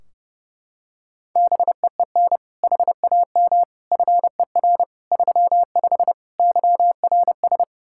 Morse code, heard as a keyed tone, about 30 wpm, 700 Hz. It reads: BEEN HAM FER 35 YRS
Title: Practice QSO - Exchange 3 - More Info - 30wpm